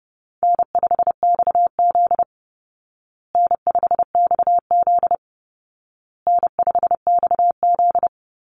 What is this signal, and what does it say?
Morse code, heard as a keyed tone, about 30 wpm, 700 Hz. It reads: N5XZ N5XZ N5XZ
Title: Easy Call Signs Repeated 3x - 30wpm